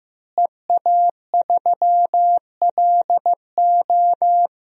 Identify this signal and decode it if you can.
Morse code, heard as a keyed tone, about 15 wpm, 700 Hz. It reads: EA3LO
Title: DXCC-100 Call Signs - 5 Characters - Repeated 3x - 15wpm